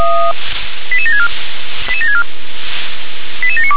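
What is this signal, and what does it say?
Five-tone selective call (ZVEI), heard as quick runs of five tones, tones of about 70 ms. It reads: 88764 88764 88792
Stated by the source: ZVEI five-tone selective call 88727 received by SDR.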